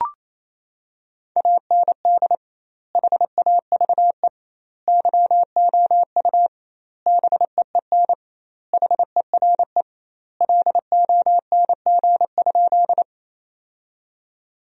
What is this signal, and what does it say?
Morse code, heard as a keyed tone, about 28 wpm, 700 Hz. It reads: AND HAVE YOU BEEN HERE LONG?
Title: 2x Word-Spacing - Sentences from Top 200 Words - 28wpm